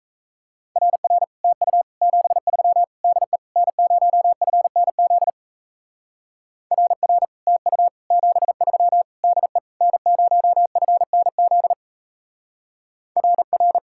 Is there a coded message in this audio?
Yes: Morse code from 0.8 to 13.8 s, heard as a keyed tone, about 36 wpm, 700 Hz. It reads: RR TU 73 DE N0FNZ RR TU 73 DE N0FNZ RR